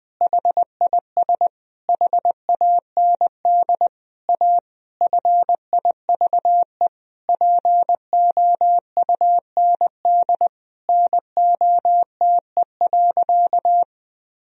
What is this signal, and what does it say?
Morse code, heard as a keyed tone, about 20 wpm, 700 Hz. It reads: HIS HAND A FIVE POUND NOTE.
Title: Sentences from Top 1000 Words - Review Part 2 - 20wpm